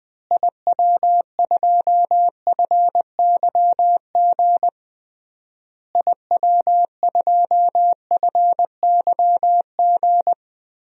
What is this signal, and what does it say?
Morse code, heard as a keyed tone, about 20 wpm, 700 Hz. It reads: IW2FYG IW2FYG
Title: DXCC-100 Call Signs - 6 Characters - Repeated 3x - 20wpm